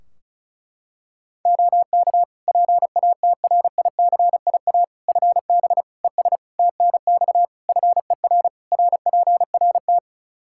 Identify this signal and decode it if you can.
Morse code, heard as a keyed tone, about 36 wpm, 700 Hz. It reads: OK PATRICIA FB ES TNX FER RPRT
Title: Practice QSO - Exchange 2 - Info - 35wpm